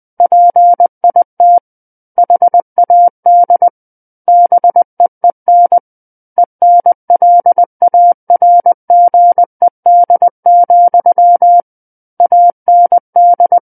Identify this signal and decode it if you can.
Morse code, heard as a keyed tone, about 20 wpm, 700 Hz. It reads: PIT HAD BEEN ENLARGED, AND
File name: 11-Chapter 11 - 20 WPM